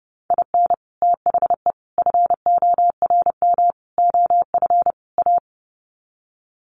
Morse code, heard as a keyed tone, about 30 wpm, 700 Hz. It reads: IN THE FORM OF A
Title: Sets of 5 Words 30wpm